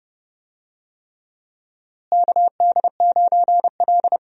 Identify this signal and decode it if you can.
Morse code, heard as a keyed tone, about 29 wpm, 700 Hz. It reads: KD9L
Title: USA Call Signs - 4 Characters - Repeated 3x - 30wpm